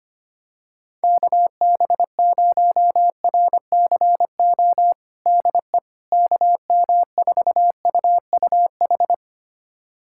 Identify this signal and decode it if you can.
Morse code, heard as a keyed tone, about 25 wpm, 700 Hz. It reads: KB0RCO DE KM4UUH
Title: USA Callsign DE Callsign - 6 Characters - 25wpm